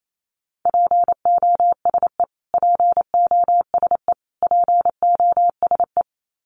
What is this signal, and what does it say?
Morse code, heard as a keyed tone, about 28 wpm, 700 Hz. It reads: POSE POSE POSE
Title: Top 500 Words Repeated 28wpm